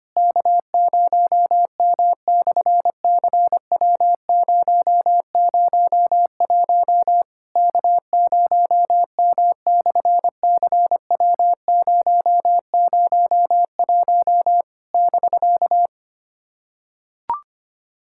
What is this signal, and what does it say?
Morse code, heard as a keyed tone, about 24 wpm, 700 Hz. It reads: K0M/CW001 K0M/CW001 <BK>
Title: SOTA QSO Practice 25wpm